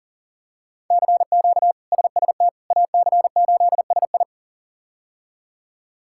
Morse code, heard as a keyed tone, about 40 wpm, 700 Hz. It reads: CQ SST AC8SI
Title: K1USN SST QSO Practice - 40wpm